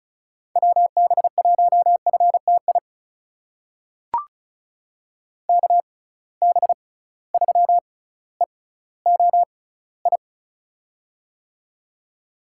Morse code, heard as a keyed tone, about 36 wpm, spaced as for 18 wpm, 700 Hz. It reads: WB1FTI KB3EOI
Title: USA Call Signs - 6 Characters - 6x Character Spacing - 35wpm